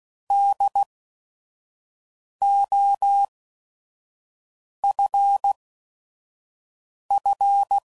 Morse code, heard as a keyed tone, about 16 wpm, spaced as for 5 wpm, 800 Hz. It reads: DOFF
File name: Prefixes / By First Letter / Ds